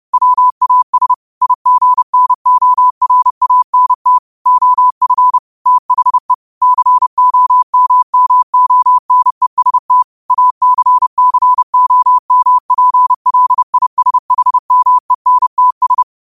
Morse code, heard as a keyed tone, about 31 wpm, 1000 Hz. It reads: WAS IGNORANT OF THE COMMONEST ACCOMPLISHMENTS
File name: qotdfiles / 30wpm